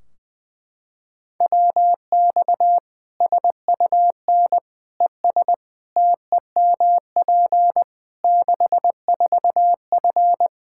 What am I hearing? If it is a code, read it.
Morse code, heard as a keyed tone, about 20 wpm, 700 Hz. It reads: WX SUN ES TEMP 64F
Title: Practice QSO - Exchange 2 - Info - 20wpm